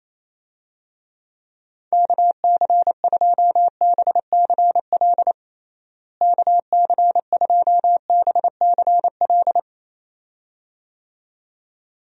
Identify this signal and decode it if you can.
Morse code, heard as a keyed tone, about 28 wpm, 700 Hz. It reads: KC2BCL KC2BCL
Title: USA Call Signs - 6 Characters - Repeated 2x - 28wpm